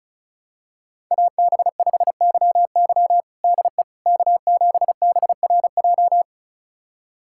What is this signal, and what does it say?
Morse code, heard as a keyed tone, about 36 wpm, 700 Hz. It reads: AB5YY DE K7BRJ
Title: USA Callsign DE Callsign - 5 Characters - 35wpm